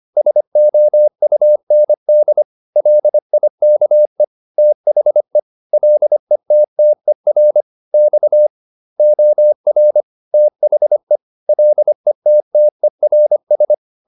Morse code, heard as a keyed tone, about 25 wpm, 600 Hz. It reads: SOUND LIKE THE LETTER X OR THE LETTERS